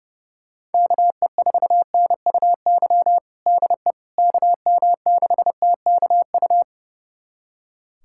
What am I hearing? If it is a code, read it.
Morse code, heard as a keyed tone, about 29 wpm, 700 Hz. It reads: KE4NUY DE KM6TKU
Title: USA Callsign DE Callsign 30wpm